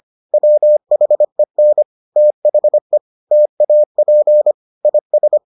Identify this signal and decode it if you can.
Morse code, heard as a keyed tone, about 25 wpm, 600 Hz. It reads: WHEN THE TAP IS